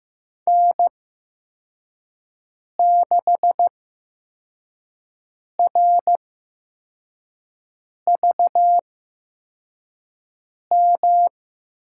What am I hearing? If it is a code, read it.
Morse code, heard as a keyed tone, about 15 wpm, spaced as for 4 wpm, 700 Hz. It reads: N6RVM
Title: USA Call Signs - 5 Characters - 8x Character Spacing - 15wpm